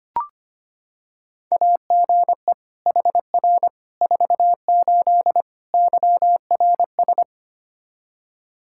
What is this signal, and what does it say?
Morse code, heard as a keyed tone, about 25 wpm, 700 Hz. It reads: AGE HR 48 YRS
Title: Practice QSO - Exchange 3 - More Info - 25wpm